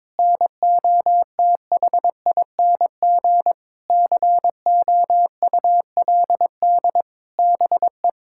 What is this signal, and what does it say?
Morse code, heard as a keyed tone, about 22 wpm, 700 Hz. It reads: NOTHING COULD BE
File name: practice / 22wpm / Sentences with 8-Word Phrases - 22wpm